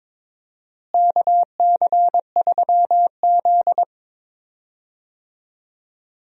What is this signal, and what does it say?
Morse code, heard as a keyed tone, about 22 wpm, 700 Hz. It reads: KC3Z
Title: USA Call Signs - 4 Characters - Repeated 3x - 22wpm